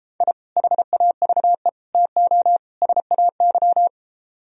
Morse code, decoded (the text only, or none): I HAVE TO SAY